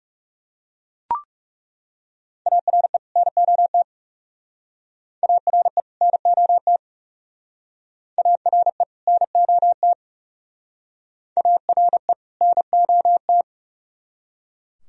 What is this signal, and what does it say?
Morse code, heard as a keyed tone, about 37 wpm, 700 Hz. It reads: ARE NOT ARE NOT ARE NOT ARE NOT
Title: Speed-Racing - Sets of 2 Words 30wpm